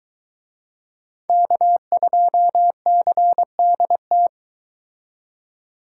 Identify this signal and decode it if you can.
Morse code, heard as a keyed tone, about 22 wpm, 700 Hz. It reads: K2CDT